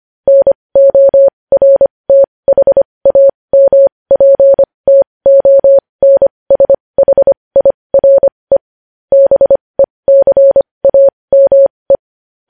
Morse code, heard as a keyed tone, about 25 wpm, spaced as for 20 wpm, 550 Hz. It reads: NORTHAMPTONSHIRE BECAME
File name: podcast / News headlines-1x-25x20